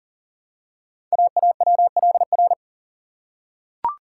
Morse code, heard as a keyed tone, about 43 wpm, 700 Hz. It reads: AAWLR
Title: Code Groups - 5 Letters - 40wpm